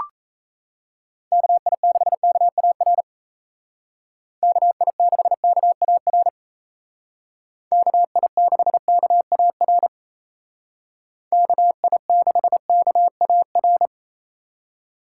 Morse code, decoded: KI6KAR KI6KAR KI6KAR KI6KAR

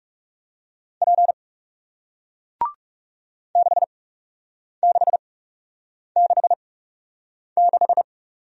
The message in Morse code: P 6666

